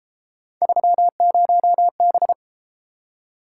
Morse code, heard as a keyed tone, about 32 wpm, 700 Hz. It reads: 30B